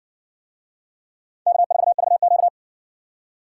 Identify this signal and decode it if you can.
Morse code, heard as a keyed tone, about 56 wpm, 700 Hz. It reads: K4VQ